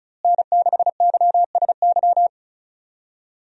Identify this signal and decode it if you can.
Morse code, heard as a keyed tone, about 34 wpm, 700 Hz. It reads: N6YSY
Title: USA Call Signs - 5 Characters - Repeated 2x - 35wpm